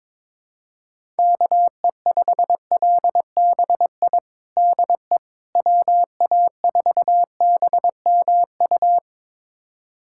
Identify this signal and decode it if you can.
Morse code, heard as a keyed tone, about 22 wpm, 700 Hz. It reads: KE5LBI DE WA4BMU
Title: USA Callsign DE Callsign - 6 Characters - 22wpm